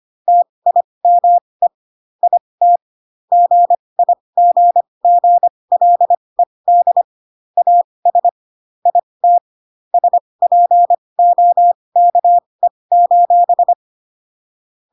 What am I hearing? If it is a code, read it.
Morse code, heard as a keyed tone, about 25 wpm, spaced as for 20 wpm, 700 Hz. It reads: TIME IT GIGGLED AS IT SPOKE:
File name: Pinocchio / Pinocchio ch1 15 25 ik1hge